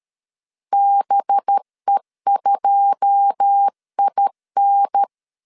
Morse code, heard as a keyed tone, about 13 wpm, 800 Hz. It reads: BE2IN
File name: LEZIONI / 10 SIGNS / 10D COMA 12wpm 2544